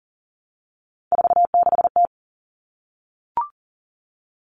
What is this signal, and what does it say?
Morse code, heard as a keyed tone, about 40 wpm, 700 Hz. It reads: 46T
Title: Common Cut Numbers - 3-Digit 40wpm